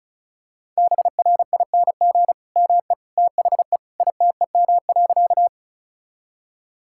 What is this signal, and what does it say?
Morse code, heard as a keyed tone, about 36 wpm, 700 Hz. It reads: BRING ME THE ITEM.
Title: Sentences from Top 1000 Words - Review Part 6 - 35wpm